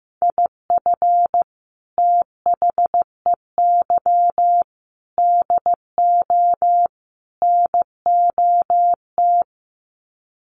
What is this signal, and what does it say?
Morse code, heard as a keyed tone, about 15 wpm, 700 Hz. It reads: IF THEY DO NOT
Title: Sets of 4 Words 15wpm